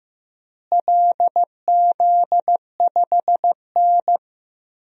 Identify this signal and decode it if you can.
Morse code, heard as a keyed tone, about 15 wpm, 700 Hz. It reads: LZ5N
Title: DXCC-100 Call Signs - 4 Characters - Repeated 2x - 15wpm